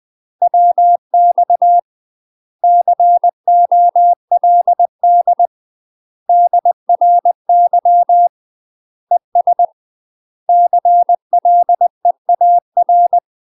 Morse code, heard as a keyed tone, about 20 wpm, 700 Hz. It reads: WX COLD DRY ES CLEAR
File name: UK MASTER QSOs TRAINING / 20WPM / UK MASTER QSOs 0051